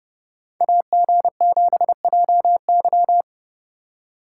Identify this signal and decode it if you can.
Morse code, heard as a keyed tone, about 30 wpm, 700 Hz. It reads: AG7JY